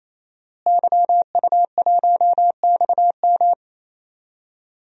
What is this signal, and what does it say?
Morse code, heard as a keyed tone, about 28 wpm, 700 Hz. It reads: YU1XM
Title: DXCC-100 Call Signs - 5 Characters - Repeated 3x - 28wpm